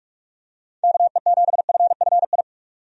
Morse code, heard as a keyed tone, about 46 wpm, 700 Hz. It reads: KE7FFI
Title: USA Call Signs - 6 Characters - 45wpm